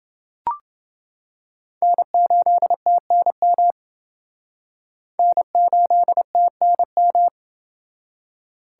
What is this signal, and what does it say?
Morse code, heard as a keyed tone, about 29 wpm, 700 Hz. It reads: N8TNM N8TNM